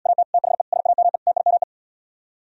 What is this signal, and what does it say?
Morse code, heard as a keyed tone, about 78 wpm, 700 Hz. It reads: AT THE SAME TIME